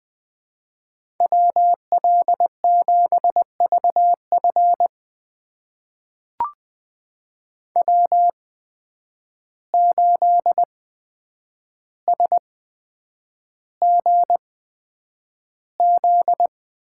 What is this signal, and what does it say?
Morse code, heard as a keyed tone, about 20 wpm, spaced as for 9 wpm, 700 Hz. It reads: WL7VF W8SGZ
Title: USA Call Signs - 5 Characters - 8x Character Spacing - 20wpm